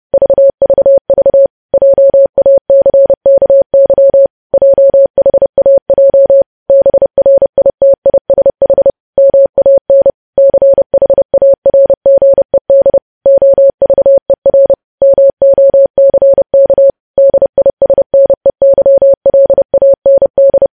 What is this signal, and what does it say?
Morse code, heard as a keyed tone, about 30 wpm, 550 Hz. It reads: VVV JACKY JHAJ BRITISH MAN CHARGED OVER MOCK DISNEYLAND
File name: podcast / News headlines-1x-30x30